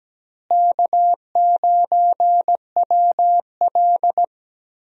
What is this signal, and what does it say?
Morse code, heard as a keyed tone, about 17 wpm, 700 Hz. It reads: K9WL
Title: USA Call Signs - 4 Characters - Repeated 2x - 17wpm